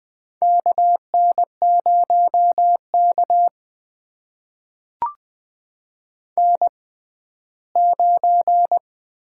Morse code, heard as a keyed tone, about 21 wpm, spaced as for 14 wpm, 700 Hz. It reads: KN0K N9
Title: USA Call Signs - 4 Characters - 6x Character Spacing - 20wpm